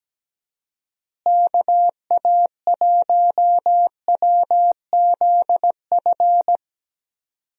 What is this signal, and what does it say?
Morse code, heard as a keyed tone, about 17 wpm, 700 Hz. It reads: KA1WZF
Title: USA Call Signs - 6 Characters - Repeated 3x - 17wpm